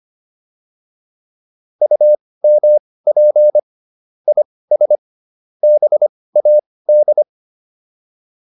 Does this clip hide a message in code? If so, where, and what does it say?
Yes: Morse code from 1.8 to 7.2 s, heard as a keyed tone, about 25 wpm, spaced as for 18 wpm, 600 Hz. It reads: UMP IS BAD